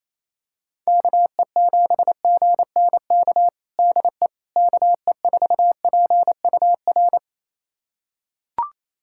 Morse code, heard as a keyed tone, about 28 wpm, 700 Hz. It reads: KE7GNK DE KE4PUR
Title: USA Callsign DE Callsign - 6 Characters - 28wpm